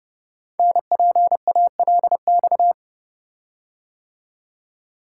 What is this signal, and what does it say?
Morse code, heard as a keyed tone, about 30 wpm, 700 Hz. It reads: NPALX
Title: Code Groups - 5 Letter:Numbers - 30wpm